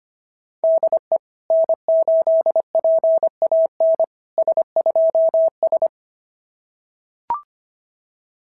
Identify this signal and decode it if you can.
Morse code, heard as a keyed tone, about 24 wpm, 650 Hz. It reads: DE N8PAN S2S
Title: SOTA QSO Practice 25wpm